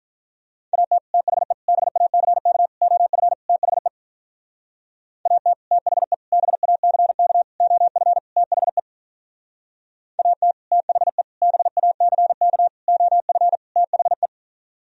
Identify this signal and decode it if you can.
Morse code, heard as a keyed tone, about 43 wpm, 700 Hz. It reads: AT THE BACK OF THE AT THE BACK OF THE AT THE BACK OF THE
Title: Speed-Racing - Sets of 5 Words 35wpm